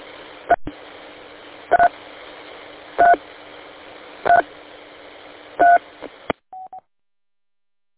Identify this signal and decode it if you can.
DTMF, heard as telephone touch tones, tones of about 90 ms. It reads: 3 3 3 3